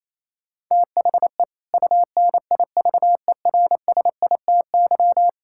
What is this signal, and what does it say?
Morse code, heard as a keyed tone, about 27 wpm, 700 Hz. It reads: THE UNIVERSITY
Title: Sentences from Top 1000 Words - Review Part 5 - 28wpm